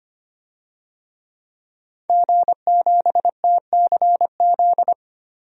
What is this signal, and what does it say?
Morse code, heard as a keyed tone, about 25 wpm, 700 Hz. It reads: G7TCZ
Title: DXCC-100 Call Signs - 5 Characters - 25wpm